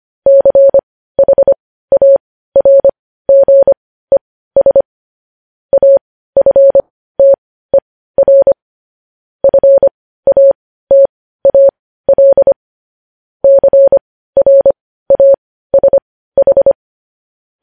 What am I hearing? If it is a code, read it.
Morse code, heard as a keyed tone, about 25 wpm, spaced as for 15 wpm, 550 Hz. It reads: CHARGES AFTER FATAL CRASH
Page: news headlines in morse code at 15 wpm